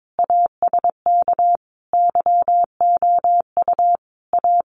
Morse code, heard as a keyed tone, about 22 wpm, 700 Hz. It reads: ASK YOU A
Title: Sentences from Top 400 Words 22wpm